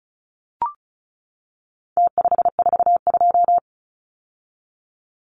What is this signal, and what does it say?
Morse code, heard as a keyed tone, about 34 wpm, 700 Hz. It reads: T542
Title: Numbers - Contest Style - 4-Digit 35wpm